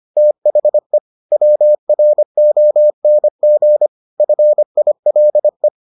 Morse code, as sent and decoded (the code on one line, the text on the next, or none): - .... / .-- .-. --- -. --. / ..-. .. .-.. .
THE WRONG FILE